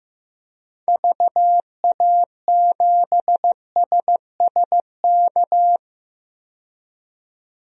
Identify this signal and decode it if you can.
Morse code, heard as a keyed tone, about 15 wpm, 700 Hz. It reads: VA7SSK
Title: DXCC-100 Call Signs - 6 Characters - 15wpm